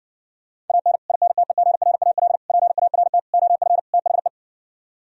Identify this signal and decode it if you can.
Morse code, heard as a keyed tone, about 59 wpm, 700 Hz. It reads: AN INTEGRAL PART OF THE